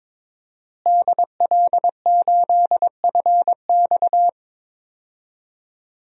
Morse code, heard as a keyed tone, about 22 wpm, 700 Hz. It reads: DL8FX
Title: DXCC-100 Call Signs - 5 Characters - Repeated 3x - 22wpm